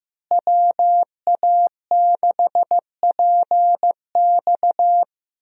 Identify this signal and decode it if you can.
Morse code, heard as a keyed tone, about 15 wpm, 700 Hz. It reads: WA6PX
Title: USA Call Signs - 5 Characters - Repeated 3x - 15wpm